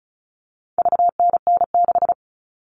Morse code, heard as a keyed tone, about 34 wpm, 700 Hz. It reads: VNN6
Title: All Cut Numbers - 4-Digit 35wpm